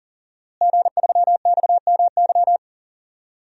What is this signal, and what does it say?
Morse code, heard as a keyed tone, about 37 wpm, 700 Hz. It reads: G3XMY
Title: DXCC-100 Call Signs - 5 Characters - Repeated 2x - 40wpm